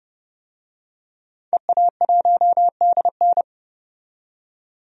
Morse code, heard as a keyed tone, about 31 wpm, 700 Hz. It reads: EA1DN
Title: DXCC-100 Call Signs - 5 Characters - 30wpm